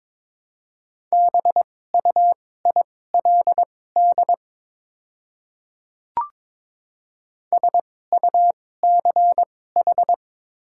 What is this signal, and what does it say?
Morse code, heard as a keyed tone, about 22 wpm, spaced as for 16 wpm, 700 Hz. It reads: BUILD SUCH
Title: 2x Character Spacing - Top 500 Words - 22wpm